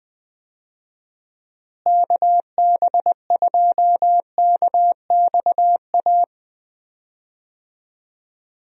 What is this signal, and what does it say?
Morse code, heard as a keyed tone, about 21 wpm, 700 Hz. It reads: KB2KXA
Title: USA Call Signs - 6 Characters - Repeated 3x - 20wpm